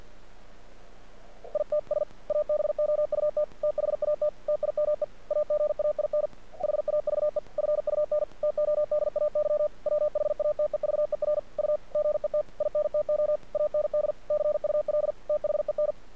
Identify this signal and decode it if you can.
Morse code, heard as a keyed tone, about 50 wpm, 600 Hz. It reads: ITS ABOUT THAT TIME AGAIN HAVE FUN TODAY WHATEVER U GET INTO AND CUL THEN
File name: archived / CRSnet